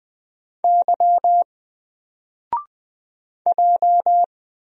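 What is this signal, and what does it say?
Morse code, heard as a keyed tone, about 20 wpm, 700 Hz. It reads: Y J
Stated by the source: Single letter and number Morse drill at 20 wpm, from N4LCD.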